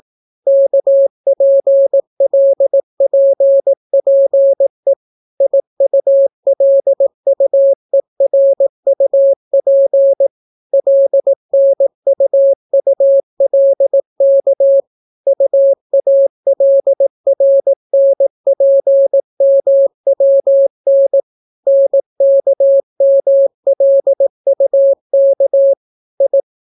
Morse code, heard as a keyed tone, about 18 wpm, 550 Hz. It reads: KPLPPE IULUERUP LNUULK UALRNPMWN NKMLUK I